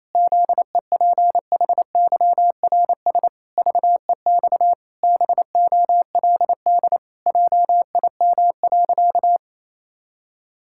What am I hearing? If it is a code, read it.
Morse code, heard as a keyed tone, about 28 wpm, 700 Hz. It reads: ZEPHYRS VEX BOLD JIM.
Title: Pangram Challenge v2 - 28wpm